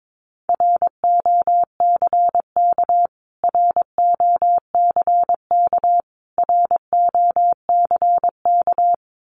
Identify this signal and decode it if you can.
Morse code, heard as a keyed tone, about 22 wpm, 700 Hz. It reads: ROCK ROCK ROCK
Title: Top 500 Words Repeated 22wpm